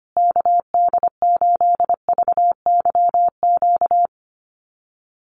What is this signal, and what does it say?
Morse code, heard as a keyed tone, about 25 wpm, 700 Hz. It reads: KD8VYQ
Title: USA Call Signs Repeated 2x 25wpm